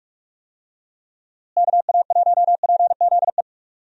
Morse code, heard as a keyed tone, about 46 wpm, 700 Hz. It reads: KA1PZE
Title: USA Call Signs - 6 Characters - Repeated 3x - 45wpm